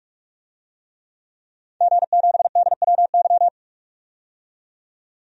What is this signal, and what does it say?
Morse code, heard as a keyed tone, about 47 wpm, 700 Hz. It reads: G7DWY
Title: DXCC-100 Call Signs - 5 Characters - 45wpm